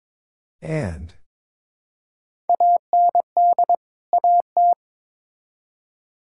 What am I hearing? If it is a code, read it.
Morse code, heard as a keyed tone, about 22 wpm, 700 Hz. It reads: AND AT
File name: practice / 22wpm / Build A Phrase - 5-Word Phrases - 22wpm